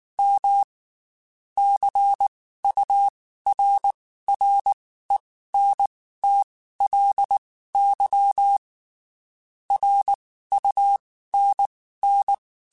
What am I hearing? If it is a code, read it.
Morse code, heard as a keyed tone, about 19 wpm, spaced as for 14 wpm, 800 Hz. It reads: M CURRENTLY RUNN
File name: QSOs / QSOs-mp3s / WD0FBT 1@18